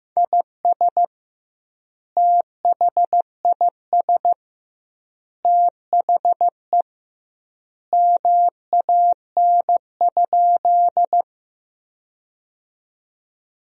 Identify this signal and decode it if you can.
Morse code, heard as a keyed tone, about 15 wpm, 700 Hz. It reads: IS THIS THE MAN?
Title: 2x Word-Spacing - Sentences from Top 200 Words - 15wpm